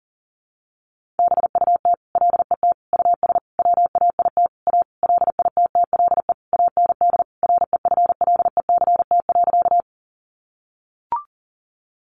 Morse code, heard as a keyed tone, about 40 wpm, 700 Hz. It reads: BUT LET US WAIT A LITTLE AND REFLECT.
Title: Sentences from Top 1500 Words 40wpm